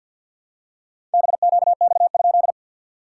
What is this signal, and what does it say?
Morse code, heard as a keyed tone, about 56 wpm, 700 Hz. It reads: BQX?
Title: SOTA QSO Practice 50wpm